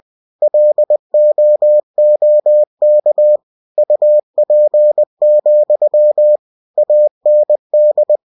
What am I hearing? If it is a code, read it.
Morse code, heard as a keyed tone, about 20 wpm, 600 Hz. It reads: LOOK UP, AND
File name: cw-books / winnie-the-pooh / winnie-20wpm-page-0021